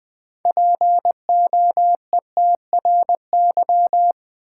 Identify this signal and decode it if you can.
Morse code, heard as a keyed tone, about 20 wpm, 700 Hz. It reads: POETRY
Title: Six Letter Words - 20wpm